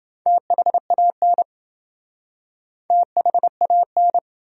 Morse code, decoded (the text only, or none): THAN THAN